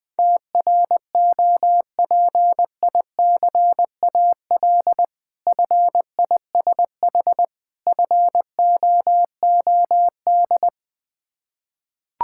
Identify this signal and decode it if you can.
Morse code, heard as a keyed tone, about 20 wpm, 690 Hz. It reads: TROPICAL FISH FOOD
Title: Difficult Words with Hints - 20wpm